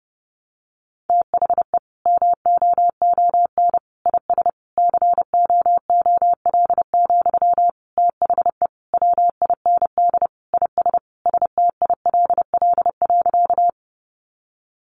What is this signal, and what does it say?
Morse code, heard as a keyed tone, about 30 wpm, 700 Hz. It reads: THE MOON IS COOL, THE WIND IS STILL.
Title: Sentences from Top 1200 Words 30wpm